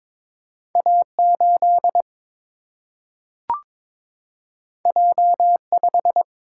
Morse code, heard as a keyed tone, about 22 wpm, 700 Hz. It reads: A 8 J 5
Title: Single-Letter Single-Number 22wpm